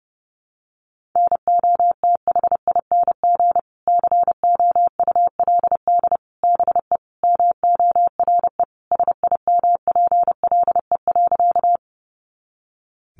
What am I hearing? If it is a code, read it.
Morse code, heard as a keyed tone, about 30 wpm, 700 Hz. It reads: NOTHING COULD BE MORE SIMPLE.
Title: Sentences from Top 500 Words 30wpm